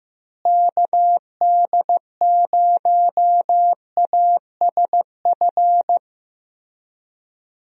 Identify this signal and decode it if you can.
Morse code, heard as a keyed tone, about 15 wpm, 700 Hz. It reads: KD0ASF
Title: USA Call Signs - 6 Characters - Repeated 3x - 15wpm